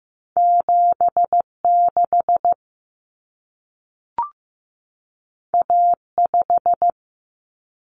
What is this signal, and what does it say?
Morse code, heard as a keyed tone, about 15 wpm, 700 Hz. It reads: 76 A5
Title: Common Cut Numbers - 2-Digit 15wpm